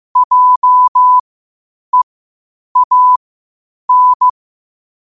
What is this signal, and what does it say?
Morse code, heard as a keyed tone, about 15 wpm, spaced as for 8 wpm, 1000 Hz. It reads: JEAN
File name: qotdfiles / 8wpm